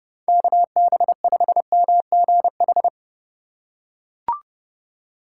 Morse code, heard as a keyed tone, about 30 wpm, 700 Hz. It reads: KB5MGH
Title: USA Call Signs - 6 Characters - 3x Character Spacing - 30wpm